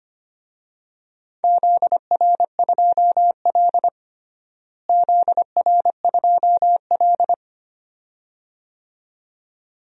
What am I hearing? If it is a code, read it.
Morse code, heard as a keyed tone, about 25 wpm, 700 Hz. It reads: ZR2L ZR2L
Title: DXCC-100 Call Signs - 4 Characters - Repeated 2x - 25wpm